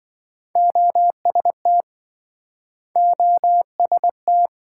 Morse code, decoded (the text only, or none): OST OST